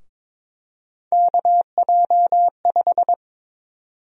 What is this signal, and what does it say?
Morse code, heard as a keyed tone, about 22 wpm, 700 Hz. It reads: KJ5
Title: USA Call Sign Prefixes - Repeated 2x - 22wpm